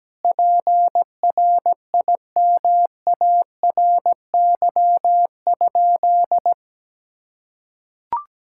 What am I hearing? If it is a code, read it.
Morse code, heard as a keyed tone, about 17 wpm, 700 Hz. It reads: PRIMARY?
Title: Sentences from Top 1000 Words - Review Part 5 - 17wpm